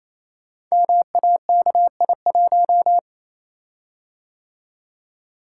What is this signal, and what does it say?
Morse code, heard as a keyed tone, about 27 wpm, 690 Hz. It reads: MAKI1
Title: Code Groups - 5 Letter:Numbers - 28wpm